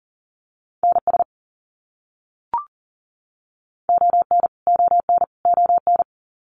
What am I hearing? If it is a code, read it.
Morse code, heard as a keyed tone, about 40 wpm, 700 Hz. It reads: NS ON ON ON
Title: Canadian Provinces and Territories - Repeated 3x - 40wpm